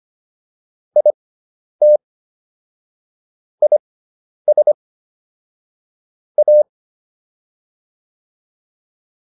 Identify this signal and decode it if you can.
Morse code, heard as a keyed tone, about 25 wpm, spaced as for 10 wpm, 600 Hz. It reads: IT IS A